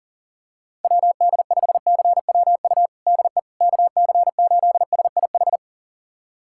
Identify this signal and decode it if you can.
Morse code, heard as a keyed tone, about 40 wpm, 700 Hz. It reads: WD5CWU DE KC8SIH
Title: USA Callsign DE Callsign 40wpm